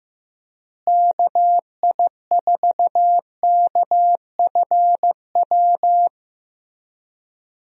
Morse code, heard as a keyed tone, about 16 wpm, 700 Hz. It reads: KI4KFW
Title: USA Call Signs - 6 Characters - Repeated 3x - 15wpm